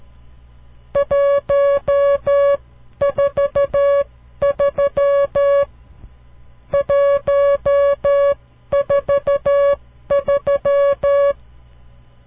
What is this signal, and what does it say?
Morse code, heard as a keyed tone, about 13 wpm, spaced as for 11 wpm, 560 Hz. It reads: 143 143
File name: CW Learning Programs / farnsworth / unit 5